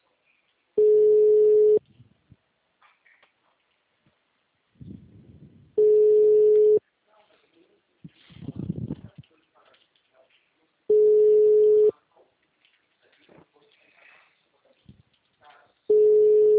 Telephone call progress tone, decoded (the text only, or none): ringback tone